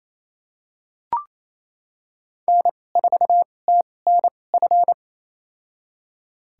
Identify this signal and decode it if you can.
Morse code, heard as a keyed tone, about 28 wpm, spaced as for 20 wpm, 700 Hz. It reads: N4TNF
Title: USA Call Signs - 5 Characters - 2x Character Spacing - 28wpm